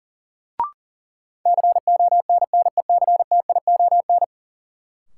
Morse code, heard as a keyed tone, about 40 wpm, 700 Hz. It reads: CONNECTION
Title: Sentences from Top 1300 Words - New Word Drill - 40wpm